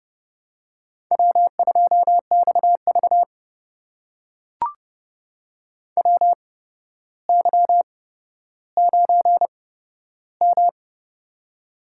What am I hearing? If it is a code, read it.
Morse code, heard as a keyed tone, about 30 wpm, spaced as for 13 wpm, 700 Hz. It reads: W2XV WY9M
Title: USA Call Signs - 4 Characters - 8x Character Spacing - 30wpm